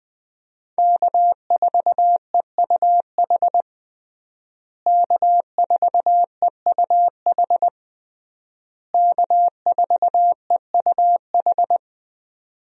Morse code, heard as a keyed tone, about 20 wpm, 700 Hz. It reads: K4EUH K4EUH K4EUH
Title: USA Call Signs - 5 Characters - Repeated 3x - 20wpm